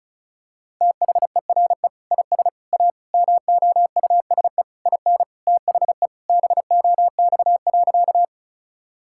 Morse code, heard as a keyed tone, about 34 wpm, 700 Hz. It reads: THERE IS A MOUSE IN THE BOX.
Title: Sentences from Top 1000 Words - Review Part 5 - 35wpm